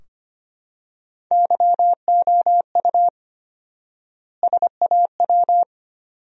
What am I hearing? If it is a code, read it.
Morse code, heard as a keyed tone, about 25 wpm, 700 Hz. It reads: YOU SAW
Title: 4x Word-Spacing - Sentences from Top 300 Words - 25wpm